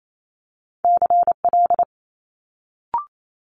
Morse code, heard as a keyed tone, about 28 wpm, 700 Hz. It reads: CL
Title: Multi-letter Phonemes Quick Fire 28wpm